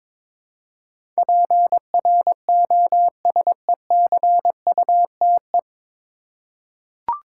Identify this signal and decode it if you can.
Morse code, heard as a keyed tone, about 22 wpm, 700 Hz. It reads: PROSECUTE
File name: 9 Letter Words - First 3 Letters Given - 22wpm